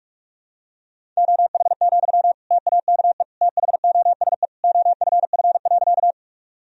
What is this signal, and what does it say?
Morse code, heard as a keyed tone, about 42 wpm, 700 Hz. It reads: OH, TAKE THOSE OFF.
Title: Sentences from Top 1000 Words - Review Part 1 - 45wpm